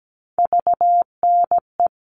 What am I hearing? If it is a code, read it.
Morse code, heard as a keyed tone, about 17 wpm, 700 Hz. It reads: VNE
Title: All Cut Numbers - 3-Digit 17wpm